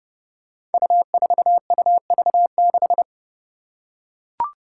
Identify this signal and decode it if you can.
Morse code, heard as a keyed tone, about 30 wpm, 700 Hz. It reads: U4UV6
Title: Five Challenging Letter-Number Combinations 30wpm